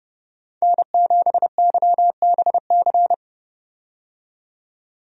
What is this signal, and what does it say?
Morse code, heard as a keyed tone, about 32 wpm, 700 Hz. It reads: N7YBC
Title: USA Call Signs - 5 Characters - 30wpm